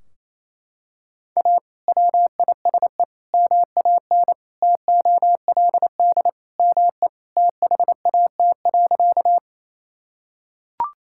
Morse code, heard as a keyed tone, about 28 wpm, 700 Hz. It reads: A WISE MAN TOLD ME THAT.
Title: Sentences from Top 1600 Words 28wpm